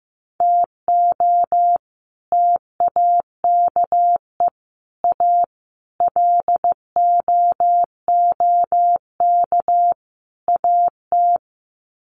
Morse code, heard as a keyed tone, about 14 wpm, 700 Hz. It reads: TO TAKE A LOOK AT
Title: Sets of 5 Words 15wpm